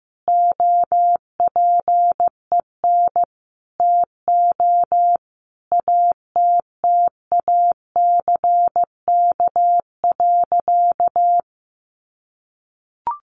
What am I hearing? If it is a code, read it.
Morse code, heard as a keyed tone, about 14 wpm, 700 Hz. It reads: OPEN TO ATTACK.
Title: Sentences from Top 1500 Words 15wpm